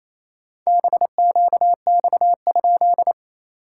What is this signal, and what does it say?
Morse code, heard as a keyed tone, about 27 wpm, 690 Hz. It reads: BQX?
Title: SOTA QSO Practice 28wpm